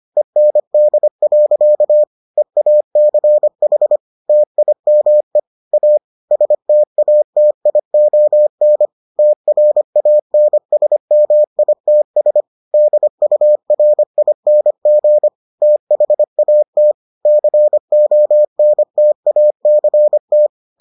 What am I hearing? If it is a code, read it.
Morse code, heard as a keyed tone, about 25 wpm, 600 Hz. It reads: END. EACH TIME A STATION TRANSMITS DURING THAT CONTACT